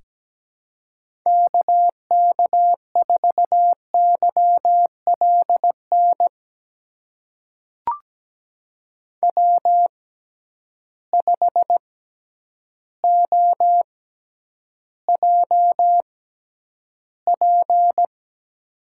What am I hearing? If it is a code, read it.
Morse code, heard as a keyed tone, about 17 wpm, spaced as for 9 wpm, 700 Hz. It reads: KK4YLN W5OJP
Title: USA Call Signs - 6x Character Spacing - 17wpm